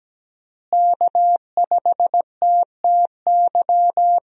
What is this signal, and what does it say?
Morse code, heard as a keyed tone, about 17 wpm, 700 Hz. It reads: K5TTY
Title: USA Call Signs - 5 Characters - Repeated 3x - 17wpm